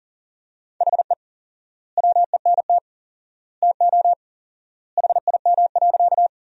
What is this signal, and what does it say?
Morse code, heard as a keyed tone, about 40 wpm, 700 Hz. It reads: HE WENT TO HIM.
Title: 4x Word-Spacing - Sentences from Top 200 Words - 40wpm